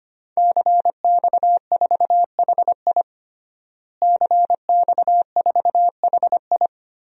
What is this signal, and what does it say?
Morse code, heard as a keyed tone, about 25 wpm, 700 Hz. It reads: CX4HI CX4HI
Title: DXCC-100 Call Signs - 5 Characters - Repeated 2x - 25wpm